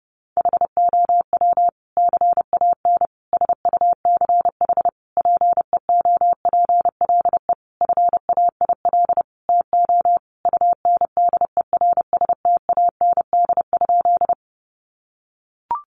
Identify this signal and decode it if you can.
Morse code, heard as a keyed tone, about 30 wpm, 700 Hz. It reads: HOW CAN SUCH PEOPLE FAIL TO UNDERSTAND?
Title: Sentences from Top 1100 Words 30wpm